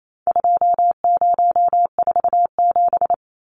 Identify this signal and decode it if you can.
Morse code, heard as a keyed tone, about 28 wpm, 700 Hz. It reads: 2047
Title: Numbers - 4-Digit 28wpm